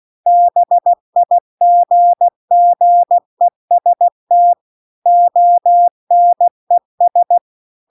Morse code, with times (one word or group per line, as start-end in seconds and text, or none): 0.3-4.5 BIGGEST
5.1-7.4 ONES